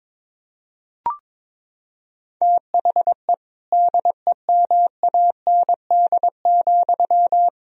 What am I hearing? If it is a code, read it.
Morse code, heard as a keyed tone, about 22 wpm, 700 Hz. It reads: THE DEMAND,
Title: Sentences from Top 1000 Words - Review Part 4 - 22wpm